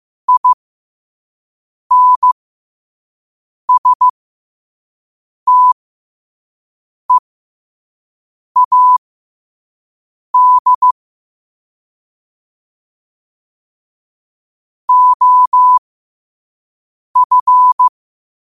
Morse code, heard as a keyed tone, about 15 wpm, spaced as for 5 wpm, 1000 Hz. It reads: INSTEAD OF
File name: 5wpm